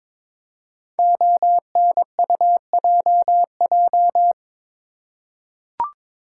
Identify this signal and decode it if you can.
Morse code, heard as a keyed tone, about 22 wpm, 700 Hz. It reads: ONUJJ
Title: Code Groups - 5 Letters - 22wpm